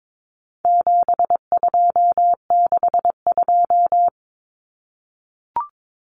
Morse code, heard as a keyed tone, about 22 wpm, 700 Hz. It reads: 7262
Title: Common Cut Numbers - 4-Digit 22wpm